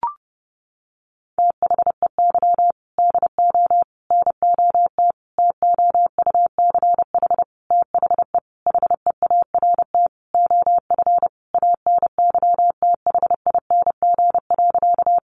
Morse code, heard as a keyed tone, about 30 wpm, 700 Hz. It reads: THEY DO NOT TOUCH THE HEART OF ANYTHING.
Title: Sentences from Top 1200 Words 30wpm